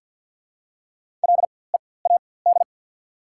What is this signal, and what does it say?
Morse code, heard as a keyed tone, about 53 wpm, spaced as for 24 wpm, 690 Hz. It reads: LEAD
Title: 4x Character Spacing - Top 500 Words - 50wpm